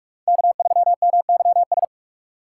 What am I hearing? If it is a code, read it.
Morse code, heard as a keyed tone, about 46 wpm, 700 Hz. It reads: K3MYS
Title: USA Call Signs - 5 Characters - 3x Character Spacing - 45wpm